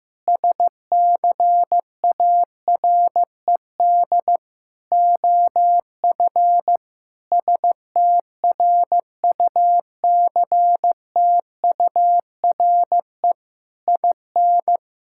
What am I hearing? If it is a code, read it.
Morse code, heard as a keyed tone, about 15 wpm, 700 Hz. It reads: SCARED OF STRUCTURE IN
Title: Sentences from Top 1000 Words - Review Part 6 - 15wpm